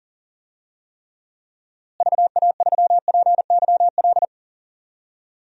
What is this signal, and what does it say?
Morse code, heard as a keyed tone, about 40 wpm, 700 Hz. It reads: VA3PYL
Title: DXCC-100 Call Signs - 6 Characters - Repeated 3x - 40wpm